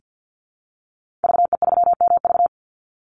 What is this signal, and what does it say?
Morse code, heard as a keyed tone, about 50 wpm, 700 Hz. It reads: VE3MV
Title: Call Signs - 50wpm